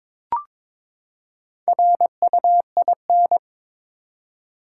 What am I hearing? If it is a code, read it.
Morse code, heard as a keyed tone, about 22 wpm, 700 Hz. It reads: RUIN
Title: Four Letter Words - 22wpm